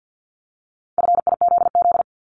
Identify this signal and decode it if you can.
Morse code, heard as a keyed tone, about 50 wpm, 700 Hz. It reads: FIZZ
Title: Multi-letter Phonemes 50wpm